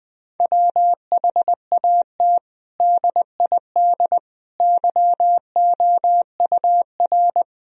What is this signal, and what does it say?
Morse code, heard as a keyed tone, about 20 wpm, 700 Hz. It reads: WHAT DID YOUR
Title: Sentences from Top 1000 Words - Review Part 4 - 20wpm